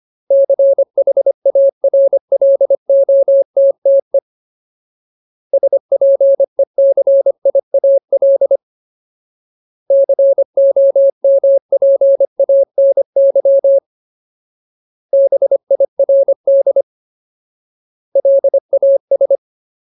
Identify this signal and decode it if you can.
Morse code, heard as a keyed tone, about 25 wpm, 550 Hz. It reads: CHARLOTTE SPECIAL COMPANY BIRD LAS
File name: Ultimate Word List / 25WPM Ultimate